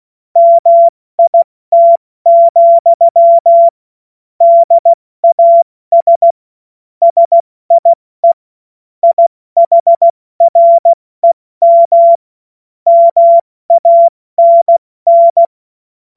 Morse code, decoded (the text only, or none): MIT, DAS SIE IHREM MANN